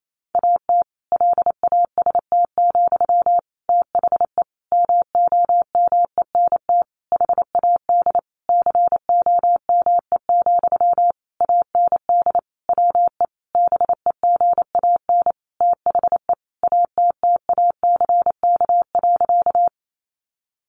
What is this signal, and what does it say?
Morse code, heard as a keyed tone, about 28 wpm, 700 Hz. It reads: AT LAST, THE MOMENT HAD COME, AND WE BEGAN THE ATTACK.
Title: Sentences from Top 1500 Words 28wpm